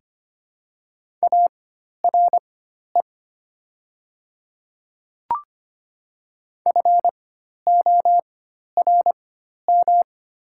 Morse code, heard as a keyed tone, about 25 wpm, spaced as for 12 wpm, 700 Hz. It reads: ARE FORM
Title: 4x Character Spacing - Top 200 Words - 25wpm